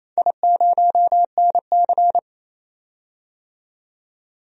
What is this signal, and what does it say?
Morse code, heard as a keyed tone, about 28 wpm, 700 Hz. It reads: I0NC
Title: DXCC-100 Call Signs - 4 Characters - 28wpm